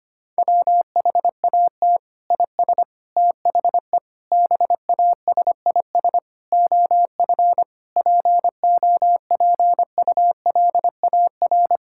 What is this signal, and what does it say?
Morse code, heard as a keyed tone, about 25 wpm, 700 Hz. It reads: WHAT IS THE BASIS OF POPULAR
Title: Sentences from Top 1000 Words - Review Part 5 - 25wpm